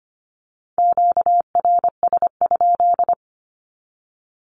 Morse code, heard as a keyed tone, about 25 wpm, 700 Hz. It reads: QRS?
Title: Q-Codes 25wpm